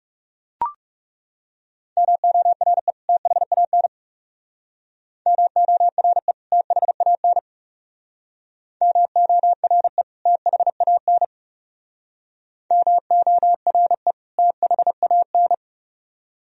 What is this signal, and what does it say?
Morse code, heard as a keyed tone, about 36 wpm, 700 Hz. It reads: MORE THAN MORE THAN MORE THAN MORE THAN
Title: Speed-Racing - Sets of 2 Words 30wpm